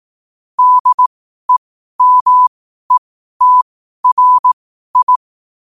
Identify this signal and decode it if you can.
Morse code, heard as a keyed tone, about 18 wpm, spaced as for 12 wpm, 1000 Hz. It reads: DEMETRI
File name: qotdfiles / 12wpm